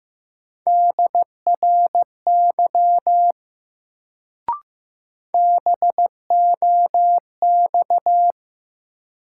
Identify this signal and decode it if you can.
Morse code, heard as a keyed tone, about 15 wpm, 700 Hz. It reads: DRY BOX
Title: Sentences from Top 500 Words - New Word Drill - 15wpm